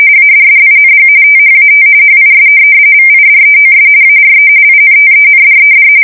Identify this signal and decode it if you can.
RTTY, about 45.45 baud. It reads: ADDING THAT HIS VIDEOS ARE INTENDED
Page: 170 Shift 2125/2295